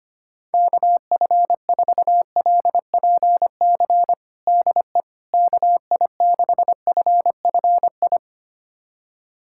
Morse code, heard as a keyed tone, about 25 wpm, 700 Hz. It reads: KF4LPC DE KI6FFI
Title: USA Callsign DE Callsign - 6 Characters - 25wpm